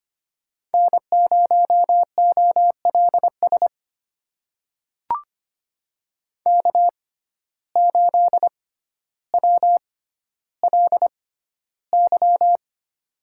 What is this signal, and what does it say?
Morse code, heard as a keyed tone, about 25 wpm, spaced as for 13 wpm, 700 Hz. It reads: N0OLS K8WLY
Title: USA Call Signs - 5 Characters - 6x Character Spacing - 25wpm